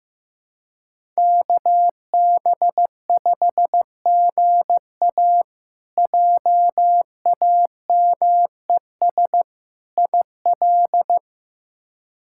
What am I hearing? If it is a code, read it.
Morse code, heard as a keyed tone, about 15 wpm, 700 Hz. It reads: KB5GA JAMES IL
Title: K1USN SST QSO Practice - 15wpm